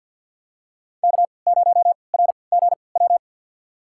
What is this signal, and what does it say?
Morse code, heard as a keyed tone, about 52 wpm, spaced as for 29 wpm, 700 Hz. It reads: K0RGW
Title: USA Call Signs - 3x Character Spacing - 50wpm